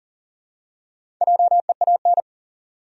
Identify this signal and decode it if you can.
Morse code, heard as a keyed tone, about 43 wpm, 700 Hz. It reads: JEAN
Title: USA Common Names - Max 4 Letters - 40wpm